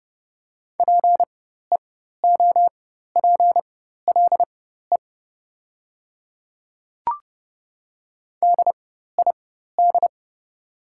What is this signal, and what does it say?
Morse code, heard as a keyed tone, about 30 wpm, spaced as for 14 wpm, 700 Hz. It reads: PEOPLE DID
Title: 4x Character Spacing - Top 100 Words - 30wpm